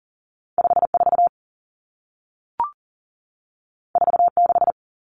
Morse code, heard as a keyed tone, about 40 wpm, 700 Hz. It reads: 54 46
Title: Numbers - 2-Digit 40wpm